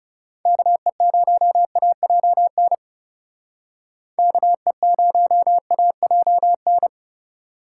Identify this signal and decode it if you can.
Morse code, heard as a keyed tone, about 38 wpm, 700 Hz. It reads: KE0AJN KE0AJN